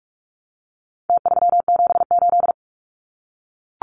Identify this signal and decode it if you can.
Morse code, heard as a keyed tone, about 44 wpm, 700 Hz. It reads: T378
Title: Numbers - Contest Style - 4-Digit 45wpm